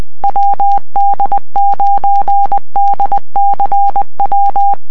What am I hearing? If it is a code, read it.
Morse code, heard as a keyed tone, about 20 wpm, 800 Hz. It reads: WD9DCW